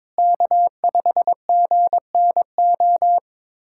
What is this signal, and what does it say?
Morse code, heard as a keyed tone, about 22 wpm, 700 Hz. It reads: K5GNO
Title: USA Callsign DE Callsign - 5 Characters - 22wpm